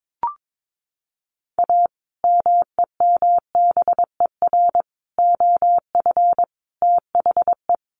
Morse code, heard as a keyed tone, about 22 wpm, 700 Hz. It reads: A MEMBER OF THE
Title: Sets of 4 Words 22wpm